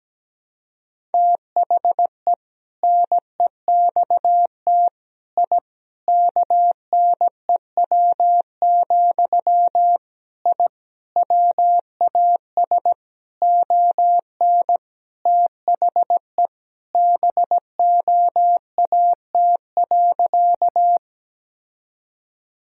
Morse code, heard as a keyed tone, about 17 wpm, 700 Hz. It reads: THE NEXT I KNEW, I WAS ON THE BOAT.
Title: Sentences from Top 1000 Words - Review Part 3 - 17wpm